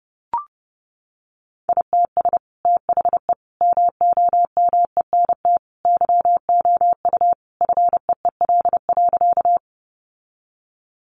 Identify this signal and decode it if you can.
Morse code, heard as a keyed tone, about 30 wpm, 700 Hz. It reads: ITS THE MOMENT YOU FEEL.
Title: Sentences from Top 1000 Words 30wpm